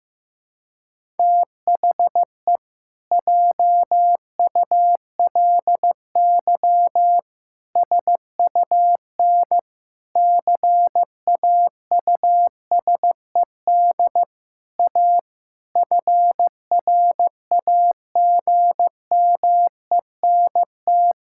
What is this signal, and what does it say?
Morse code, heard as a keyed tone, about 14 wpm, 700 Hz. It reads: THE JULY SUN CAUSED A FRAGMENT
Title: Pangram Challenge v2 - 15wpm